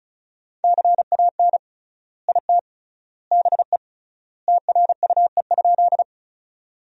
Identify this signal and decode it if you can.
Morse code, heard as a keyed tone, about 36 wpm, 700 Hz. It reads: CAN IT BE TRUE?
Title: 3x Word-Spacing - Sentences from Top 400 Words - 35wpm